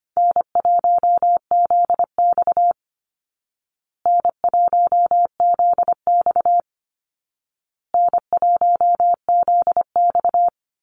Morse code, heard as a keyed tone, about 25 wpm, 700 Hz. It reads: N1ZX N1ZX N1ZX
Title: Easy Call Signs Repeated 3x - 25wpm